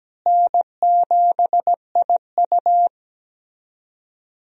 Morse code, heard as a keyed tone, about 17 wpm, 700 Hz. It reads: N7IU
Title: USA Call Signs - 4 Characters - Repeated 3x - 17wpm